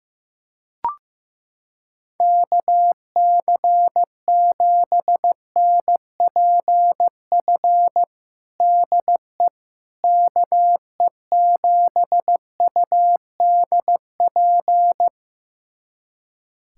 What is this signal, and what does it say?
Morse code, heard as a keyed tone, about 15 wpm, 700 Hz. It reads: KC7NPF DE KE7UDP
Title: USA Callsign DE Callsign - 6 Characters - 15wpm